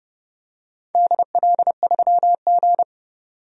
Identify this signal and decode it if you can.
Morse code, heard as a keyed tone, about 30 wpm, 700 Hz. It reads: DL3G
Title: DXCC-100 Call Signs - 4 Characters - 30wpm